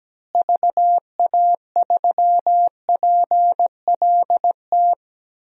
Morse code, heard as a keyed tone, about 17 wpm, 700 Hz. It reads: VA3PLT
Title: DXCC-100 Call Signs - 6 Characters - Repeated 3x - 17wpm